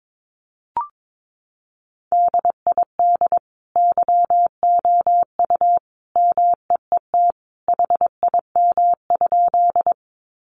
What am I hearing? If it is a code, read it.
Morse code, heard as a keyed tone, about 22 wpm, 700 Hz. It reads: DID YOU MEET HIM?
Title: Sentences from Top 1100 Words 22wpm